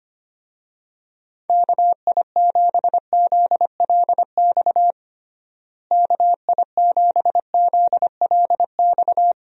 Morse code, heard as a keyed tone, about 25 wpm, 700 Hz. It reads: KI7ZLX KI7ZLX
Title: USA Call Signs - 6 Characters - Repeated 2x - 25wpm